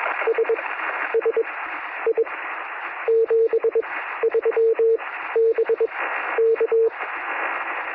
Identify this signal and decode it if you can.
Morse code, heard as a keyed tone, about 21 wpm, spaced as for 11 wpm, 450 Hz, with noise. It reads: SSI 73 BK
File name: sm7lcb / 1957z oh5lk 23cm